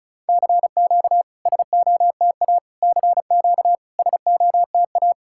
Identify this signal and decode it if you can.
Morse code, heard as a keyed tone, about 36 wpm, 690 Hz. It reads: CQ SOTA CQ SOTA
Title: SOTA QSO Practice 35wpm